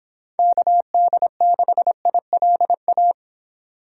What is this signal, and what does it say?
Morse code, heard as a keyed tone, about 29 wpm, 700 Hz. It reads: KD6ILA